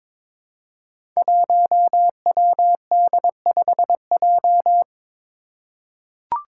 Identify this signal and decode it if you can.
Morse code, heard as a keyed tone, about 22 wpm, 700 Hz. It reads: 1WD5J
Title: Five Challenging Letter-Number Combinations 22wpm